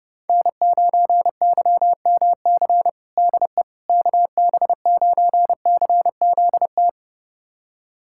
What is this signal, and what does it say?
Morse code, heard as a keyed tone, about 30 wpm, 700 Hz. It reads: N9YMC DE KB9CZT
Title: USA Callsign DE Callsign 30wpm